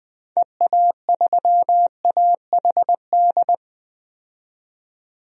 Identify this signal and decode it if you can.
Morse code, heard as a keyed tone, about 20 wpm, 700 Hz. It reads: EA3AHD
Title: DXCC-100 Call Signs - 6 Characters - 20wpm